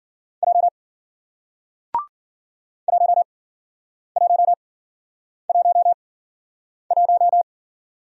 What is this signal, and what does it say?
Morse code, heard as a keyed tone, about 52 wpm, spaced as for 9 wpm, 700 Hz. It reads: J 1111